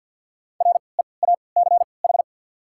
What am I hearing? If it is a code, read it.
Morse code, heard as a keyed tone, about 51 wpm, spaced as for 28 wpm, 690 Hz. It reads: REACH
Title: 3x Character Spacing - Top 500 Words - 50wpm